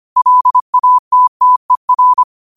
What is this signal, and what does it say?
Morse code, heard as a keyed tone, about 25 wpm, 1000 Hz. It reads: LATTER
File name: qotdfiles / 25wpm